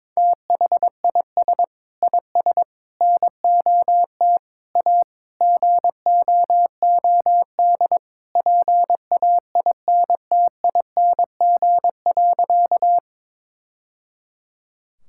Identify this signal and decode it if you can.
Morse code, heard as a keyed tone, about 22 wpm, 700 Hz. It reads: THIS IS NOT A GOOD PAINTING.
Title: Sentences from Top 1000 Words - Review Part 6 - 22wpm